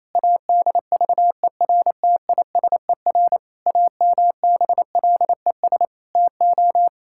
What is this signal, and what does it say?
Morse code, heard as a keyed tone, about 28 wpm, 700 Hz. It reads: ADVERTISER AMBLES TO
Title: Pangram Challenge v2 - 28wpm